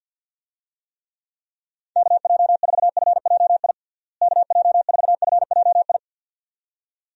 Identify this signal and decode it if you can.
Morse code, heard as a keyed tone, about 46 wpm, 700 Hz. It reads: KJ4FJI KJ4FJI
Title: USA Call Signs - 6 Characters - Repeated 2x - 50wpm